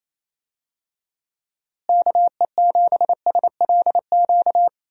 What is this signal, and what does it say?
Morse code, heard as a keyed tone, about 28 wpm, 700 Hz. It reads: KE7SLQ
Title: USA Call Signs - 6 Characters - Repeated 3x - 28wpm